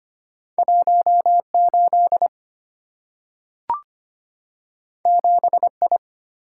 Morse code, heard as a keyed tone, about 25 wpm, 690 Hz. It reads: 18 7I
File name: Two Letter-Number Combinations - 25wpm